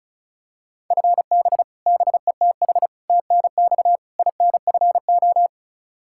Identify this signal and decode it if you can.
Morse code, heard as a keyed tone, about 36 wpm, 700 Hz. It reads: FB BETH TNX INFO
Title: Practice QSO - Exchange 3 - More Info - 35wpm